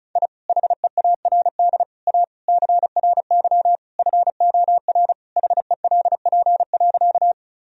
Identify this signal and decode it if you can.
Morse code, heard as a keyed tone, about 34 wpm, 700 Hz. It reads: I HEARD A CRY FOR HELP.
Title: Sentences from Top 1000 Words - Review Part 2 - 35wpm